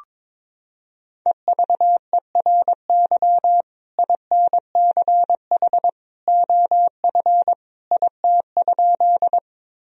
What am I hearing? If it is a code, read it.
Morse code, heard as a keyed tone, about 22 wpm, 700 Hz. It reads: EVERY INCH OF IT?
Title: Sentences from Top 1000 Words - Review Part 3 - 22wpm